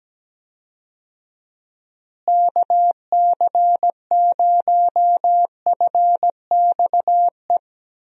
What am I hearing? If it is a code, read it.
Morse code, heard as a keyed tone, about 17 wpm, 700 Hz. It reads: KC0FXE
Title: USA Call Signs - 6 Characters - Repeated 2x - 17wpm